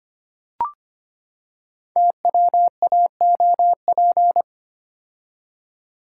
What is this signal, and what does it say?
Morse code, heard as a keyed tone, about 25 wpm, 700 Hz. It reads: TWAOP
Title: Code Groups - 5 Letters - 25wpm